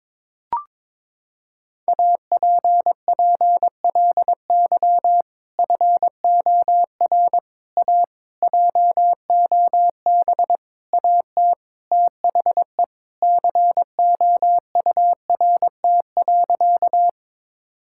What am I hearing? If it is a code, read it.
Morse code, heard as a keyed tone, about 22 wpm, 700 Hz. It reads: APPLY FOR A JOB AT THE COURT.
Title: Sentences from Top 1600 Words 22wpm